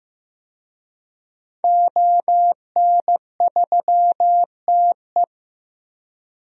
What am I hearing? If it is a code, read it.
Morse code, heard as a keyed tone, about 15 wpm, 700 Hz. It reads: ON3TE
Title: DXCC-100 Call Signs - 5 Characters - Repeated 2x - 15wpm